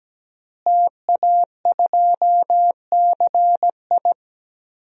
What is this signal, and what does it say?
Morse code, heard as a keyed tone, about 17 wpm, 700 Hz. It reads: TA2CI
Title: DXCC-100 Call Signs - 5 Characters - Repeated 2x - 17wpm